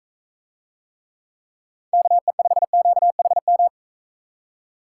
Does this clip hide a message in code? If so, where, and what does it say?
Yes: Morse code from 1.9 to 3.7 s, heard as a keyed tone, about 40 wpm, 700 Hz. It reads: KE5QHM